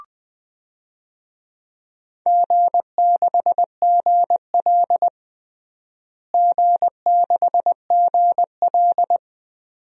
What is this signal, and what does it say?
Morse code, heard as a keyed tone, about 20 wpm, 700 Hz. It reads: G6GL G6GL
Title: DXCC-100 Call Signs - 4 Characters - Repeated 2x - 20wpm